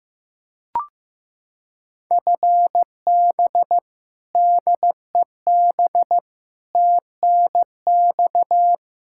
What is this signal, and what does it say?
Morse code, heard as a keyed tone, about 15 wpm, 700 Hz. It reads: FB DEB TNX
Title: Practice QSO - Exchange 3 - More Info - 15wpm